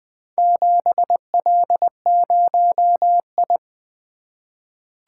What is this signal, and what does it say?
Morse code, heard as a keyed tone, about 20 wpm, 700 Hz. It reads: 7L0I
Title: Four Letter-Number Combinations - 20wpm